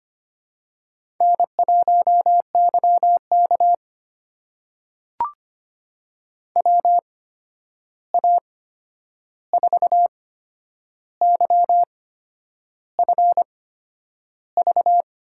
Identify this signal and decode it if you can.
Morse code, heard as a keyed tone, about 25 wpm, spaced as for 9 wpm, 700 Hz. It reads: N1YK WA4YFV
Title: USA Call Signs - 8x Character Spacing - 25wpm